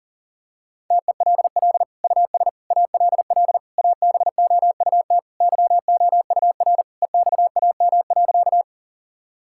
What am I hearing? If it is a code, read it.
Morse code, heard as a keyed tone, about 40 wpm, 700 Hz. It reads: TELL US ALL ABOUT YOUR EXAM.
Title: Sentences from Top 1000 Words - Review Part 5 - 40wpm